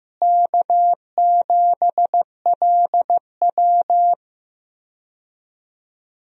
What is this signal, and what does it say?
Morse code, heard as a keyed tone, about 15 wpm, 700 Hz. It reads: K7LW
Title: USA Call Signs - 4 Characters - Repeated 3x - 15wpm